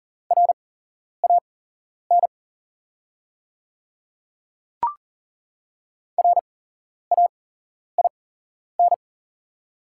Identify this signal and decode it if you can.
Morse code, heard as a keyed tone, about 39 wpm, spaced as for 11 wpm, 700 Hz. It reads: RAN RAIN
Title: 8x Character Spacing - Top 500 Words - 40wpm